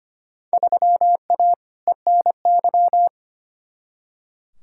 Morse code, heard as a keyed tone, about 25 wpm, 700 Hz. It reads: 3A ENY
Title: Field Day Exchanges - 25wpm